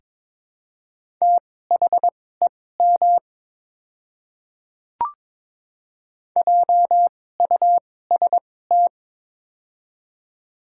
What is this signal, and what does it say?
Morse code, heard as a keyed tone, about 21 wpm, spaced as for 16 wpm, 700 Hz. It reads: THEM JUST
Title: 2x Character Spacing - Top 200 Words - 22wpm